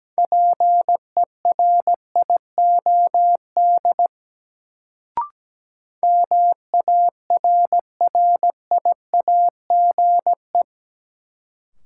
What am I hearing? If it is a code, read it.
Morse code, heard as a keyed tone, about 17 wpm, 700 Hz. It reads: PERIOD MARRIAGE
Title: Sentences from Top 800 Words - New Word Drill - 17wpm